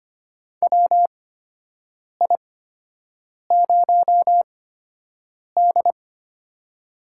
Morse code, heard as a keyed tone, about 25 wpm, spaced as for 7 wpm, 700 Hz. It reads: WI0D
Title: USA Call Signs - 4 Characters - 8x Character Spacing - 25wpm